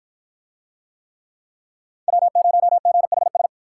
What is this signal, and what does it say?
Morse code, heard as a keyed tone, about 53 wpm, 700 Hz. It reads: W0GHS